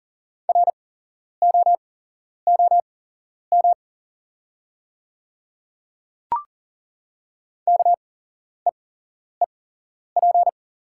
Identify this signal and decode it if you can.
Morse code, heard as a keyed tone, about 40 wpm, spaced as for 11 wpm, 700 Hz. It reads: ROOM KEEP
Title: 8x Character Spacing - Top 300 Words - 40wpm